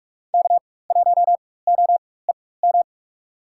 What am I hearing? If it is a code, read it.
Morse code, heard as a keyed tone, about 44 wpm, spaced as for 21 wpm, 700 Hz. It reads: K1OEM
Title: USA Call Signs - 4x Character Spacing - 45wpm